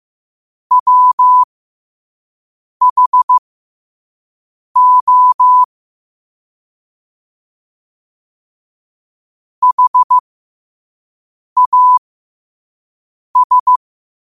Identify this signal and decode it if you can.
Morse code, heard as a keyed tone, about 15 wpm, spaced as for 5 wpm, 1000 Hz. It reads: WHO HAS
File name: 5wpm